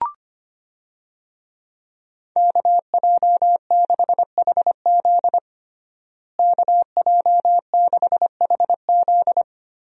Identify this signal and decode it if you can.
Morse code, heard as a keyed tone, about 25 wpm, 700 Hz. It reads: KJ6HZ KJ6HZ
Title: USA Call Signs - 5 Characters - Repeated 2x - 25wpm